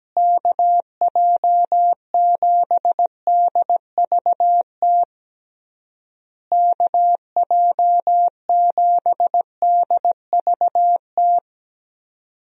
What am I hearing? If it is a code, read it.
Morse code, heard as a keyed tone, about 17 wpm, 700 Hz. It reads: KJ7DVT KJ7DVT
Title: USA Call Signs - 6 Characters - Repeated 3x - 17wpm